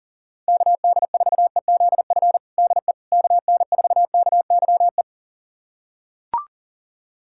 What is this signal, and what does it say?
Morse code, heard as a keyed tone, about 37 wpm, 700 Hz. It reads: KD4EZF DE KN4KYE
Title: USA Callsign DE Callsign 40wpm